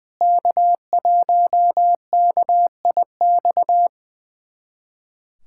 Morse code, heard as a keyed tone, about 21 wpm, 700 Hz. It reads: K1KIX
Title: USA Callsign DE Callsign - 5 Characters - 20wpm